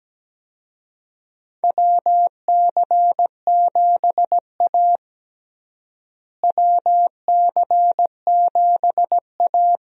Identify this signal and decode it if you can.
Morse code, heard as a keyed tone, about 17 wpm, 700 Hz. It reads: WC7A WC7A
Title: USA Call Signs - 4 Characters - Repeated 2x - 17wpm